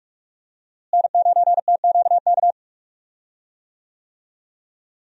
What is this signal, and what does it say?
Morse code, heard as a keyed tone, about 46 wpm, 700 Hz. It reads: N9TQK
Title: USA Call Signs - 5 Characters - Repeated 3x - 45wpm